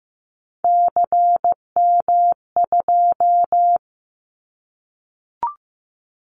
Morse code, heard as a keyed tone, about 15 wpm, 700 Hz. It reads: CM2
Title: Callsign Prefixes for Top 100 DXCC - 15wpm